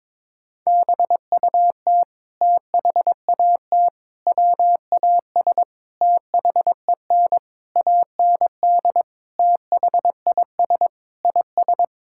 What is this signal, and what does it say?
Morse code, heard as a keyed tone, about 22 wpm, 700 Hz. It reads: BUT THAT WAS THEN AND THIS IS
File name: practice / 22wpm / Sentences with 8-Word Phrases - 22wpm